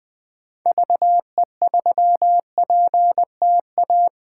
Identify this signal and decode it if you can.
Morse code, heard as a keyed tone, about 20 wpm, 700 Hz. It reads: VE3PTA
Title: 2023 POTA Hunter Call Signs - 20wpm